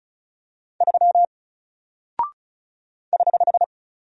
Morse code, heard as a keyed tone, about 36 wpm, 700 Hz. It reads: 3 <HH>
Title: Single Character (Letter:Number:Punctuation:Prosign) 35wpm